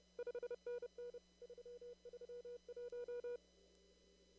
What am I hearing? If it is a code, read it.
Morse code, heard as a keyed tone, about 30 wpm, 480 Hz. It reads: 5NN 331
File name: CW / kt5j wpx cw-op-k5pi